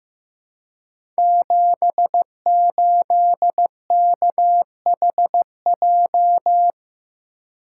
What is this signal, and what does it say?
Morse code, heard as a keyed tone, about 15 wpm, 700 Hz. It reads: 78KHJ
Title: Code Groups - 5 Letter:Numbers - 15wpm